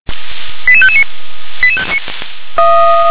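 Five-tone selective call (ZVEI), heard as quick runs of five tones, tones of about 70 ms. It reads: 88559
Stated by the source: ZVEI five-tone selective call received on an SDR.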